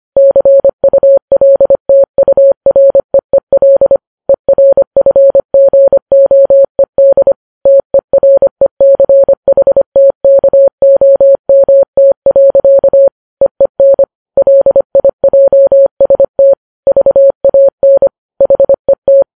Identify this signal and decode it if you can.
Morse code, heard as a keyed tone, about 25 wpm, 550 Hz. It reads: CULTUREEL ERFGOED TERECHTKOMT. EEN LIJST VAN HET
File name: Files Web tls / Morse on9fv / morse25kps